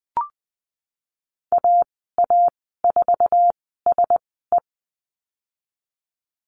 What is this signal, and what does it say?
Morse code, heard as a keyed tone, about 20 wpm, spaced as for 15 wpm, 700 Hz. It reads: AA4SE
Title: 147 Lesson 24g - Letter y - Call Signs 20wpm